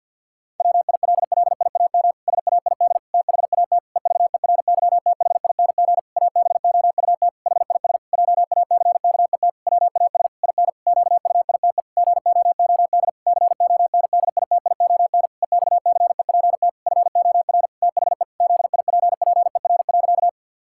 Morse code, heard as a keyed tone, about 50 wpm, 700 Hz. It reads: WILLIAM SAID THAT EVERYTHING ABOUT HIS JACKET WAS IN QUITE GOOD CONDITION EXCEPT FOR THE ZIPPER.
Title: Pangram Challenge v2 - 50wpm